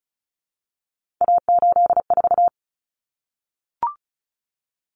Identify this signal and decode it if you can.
Morse code, heard as a keyed tone, about 34 wpm, 700 Hz. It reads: A84
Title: Common Cut Numbers - 3-Digit 35wpm